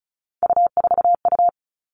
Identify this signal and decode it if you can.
Morse code, heard as a keyed tone, about 36 wpm, 700 Hz. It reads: U4U
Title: Three Challenging Letter-Number Combinations 35wpm